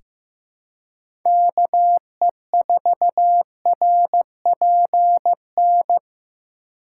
Morse code, heard as a keyed tone, about 15 wpm, 700 Hz. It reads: KE4RPN
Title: USA Call Signs - 6 Characters - 4x Character Spacing - 15wpm